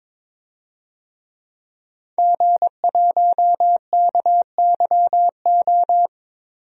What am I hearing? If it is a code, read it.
Morse code, heard as a keyed tone, about 22 wpm, 700 Hz. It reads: G1KYO
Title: DXCC-100 Call Signs - 5 Characters - 22wpm